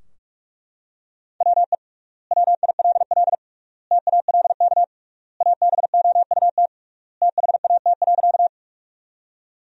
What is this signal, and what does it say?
Morse code, heard as a keyed tone, about 41 wpm, 700 Hz. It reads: WE WILL TALK ABOUT THAT.
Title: 3x Word-Spacing - Sentences from Top 400 Words - 45wpm